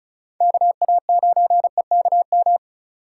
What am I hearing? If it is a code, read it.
Morse code, heard as a keyed tone, about 34 wpm, 700 Hz. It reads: KA9EKM
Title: USA Call Signs - 3x Character Spacing - 35wpm